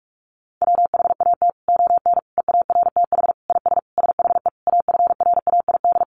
Morse code, heard as a keyed tone, about 44 wpm, 700 Hz. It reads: WHAT ON EARTH IS SHE AFRAID
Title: Sentences from Top 1500 Words 45wpm